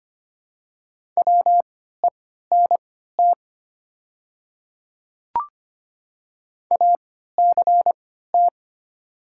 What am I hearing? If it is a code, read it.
Morse code, heard as a keyed tone, about 25 wpm, spaced as for 14 wpm, 700 Hz. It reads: WENT ACT
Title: 3x Character Spacing - Top 200 Words - 25wpm